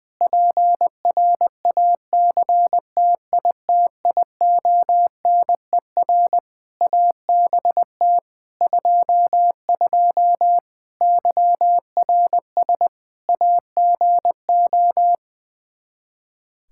Morse code, heard as a keyed tone, about 20 wpm, 700 Hz. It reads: PRACTITIONER ABT 22 YRS AGO
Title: Practice QSO - Exchange 3 - More Info - 20wpm